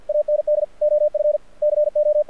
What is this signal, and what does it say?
Morse code, heard as a keyed tone, about 52 wpm, 600 Hz. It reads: AND OF CO
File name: archived / CRSnet